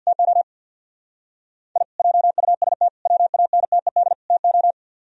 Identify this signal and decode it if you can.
Morse code, heard as a keyed tone, about 63 wpm, 700 Hz. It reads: TO I JUST WANTED TO